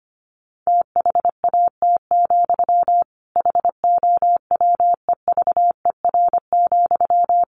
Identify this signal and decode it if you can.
Morse code, heard as a keyed tone, about 25 wpm, 700 Hz. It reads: THAT, HOWEVER,
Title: Sentences from Top 1300 Words 25wpm